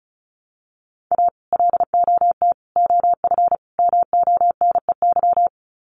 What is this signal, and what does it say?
Morse code, heard as a keyed tone, about 34 wpm, 700 Hz. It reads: A LOT OF MONEY
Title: Sets of 4 Words 35wpm